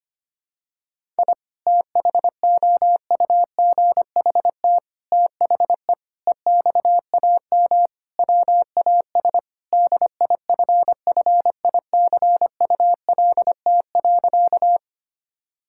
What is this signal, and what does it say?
Morse code, heard as a keyed tone, about 25 wpm, 700 Hz. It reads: I THOUGHT THE EXAM WAS DIFFICULT.
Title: Sentences from Top 1000 Words - Review Part 5 - 25wpm